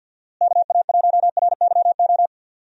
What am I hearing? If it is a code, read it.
Morse code, heard as a keyed tone, about 53 wpm, 700 Hz. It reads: KA1RYO
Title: USA Call Signs - 6 Characters - Repeated 3x - 50wpm